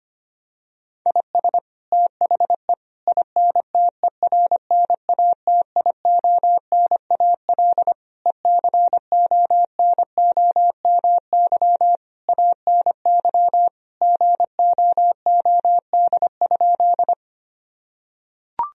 Morse code, heard as a keyed tone, about 25 wpm, 700 Hz. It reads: IS THE INTERNATIONAL ECONOMY ANY GOOD?
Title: Sentences from Top 1000 Words - Review Part 4 - 25wpm